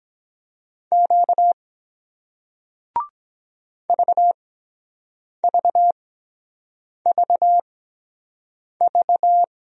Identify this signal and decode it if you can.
Morse code, heard as a keyed tone, about 23 wpm, spaced as for 7 wpm, 700 Hz. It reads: Q VVVV